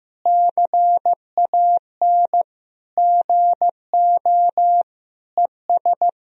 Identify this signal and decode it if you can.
Morse code, heard as a keyed tone, about 15 wpm, 700 Hz. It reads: CAN GO ES
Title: Sentences from Top 100 Words and Abbreviations 15wpm